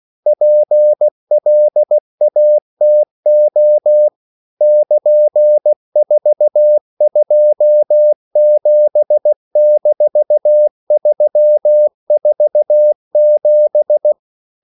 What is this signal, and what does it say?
Morse code, heard as a keyed tone, about 16 wpm, 600 Hz. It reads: PLATO <KN>427-347
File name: CW / Practice / EnglishQuotations0116